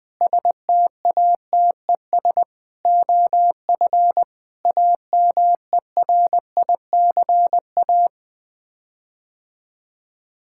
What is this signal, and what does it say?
Morse code, heard as a keyed tone, about 20 wpm, 700 Hz. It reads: STATES OF AMERICA
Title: Sets of 8 Words - 20wpm